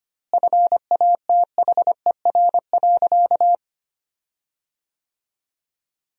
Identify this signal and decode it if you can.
Morse code, heard as a keyed tone, about 25 wpm, 700 Hz. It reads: FATHER.
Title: 4x Word-Spacing - Sentences from Top 300 Words - 25wpm